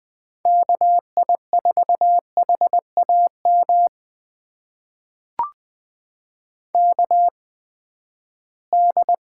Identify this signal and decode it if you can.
Morse code, heard as a keyed tone, about 21 wpm, spaced as for 14 wpm, 700 Hz. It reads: KI4HAM KD
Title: USA Call Signs - 6 Characters - 8x Character Spacing - 20wpm